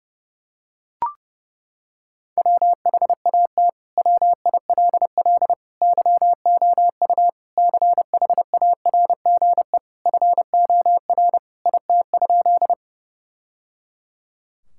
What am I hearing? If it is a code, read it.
Morse code, heard as a keyed tone, about 30 wpm, 700 Hz. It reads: WHAT WILL YOU CHARGE FOR IT?
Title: Sentences from Top 1000 Words - Review Part 6 - 30wpm